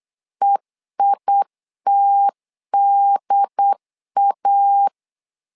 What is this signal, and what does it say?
Morse code, heard as a keyed tone, about 8 wpm, 800 Hz. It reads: EITDA
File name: LEZIONI / 3 MWRO / 3 M 8wpm 2300